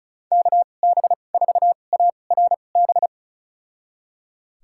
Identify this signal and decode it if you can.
Morse code, heard as a keyed tone, about 36 wpm, spaced as for 25 wpm, 700 Hz. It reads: KB4ARB
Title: USA Call Signs - 6 Characters - 2x Character Spacing - 35wpm